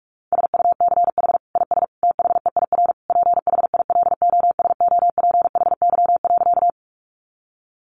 Morse code, heard as a keyed tone, about 44 wpm, 700 Hz. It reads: SUCH IS THEIR PHILOSOPHY.
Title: Sentences from Top 900 Words 45wpm